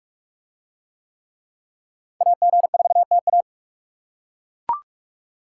Morse code, heard as a keyed tone, about 38 wpm, 700 Hz. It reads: AG4TA